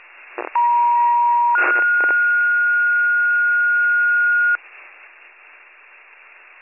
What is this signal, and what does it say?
Two-tone page, heard as a one-second tone, then a three-second tone, about 950 then 1430 Hz. It is heard from 0.6 to 4.6 s.